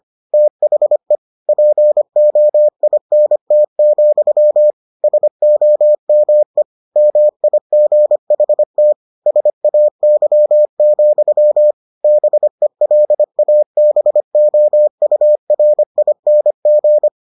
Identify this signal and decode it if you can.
Morse code, heard as a keyed tone, about 25 wpm, 600 Hz. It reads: THE POINT, SOME MIGHT SAY, BELABOURING